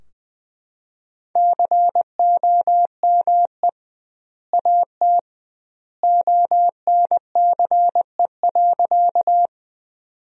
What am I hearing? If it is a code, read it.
Morse code, heard as a keyed tone, about 20 wpm, 700 Hz. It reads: COME AT ONCE.
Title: 2x Word-Spacing - Sentences from Top 300 Words - 20wpm